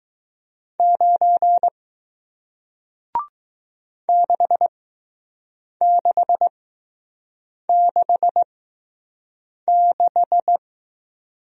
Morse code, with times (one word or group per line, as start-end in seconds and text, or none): 0.8-1.7 9
4.1-10.6 6666